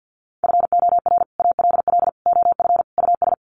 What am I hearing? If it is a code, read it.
Morse code, heard as a keyed tone, about 48 wpm, 700 Hz. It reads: FOR ALL OF US
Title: Sets of 4 Words 50wpm